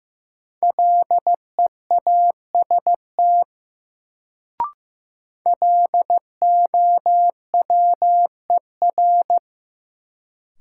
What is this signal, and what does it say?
Morse code, heard as a keyed tone, about 15 wpm, 700 Hz. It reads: LEAST LOWER
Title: Sentences from Top 700 Words - New Word Drill - 15wpm